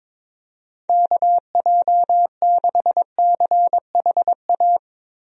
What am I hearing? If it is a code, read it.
Morse code, heard as a keyed tone, about 22 wpm, 700 Hz. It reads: KJ6CHA
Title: USA Call Signs - 6 Characters - Repeated 2x - 22wpm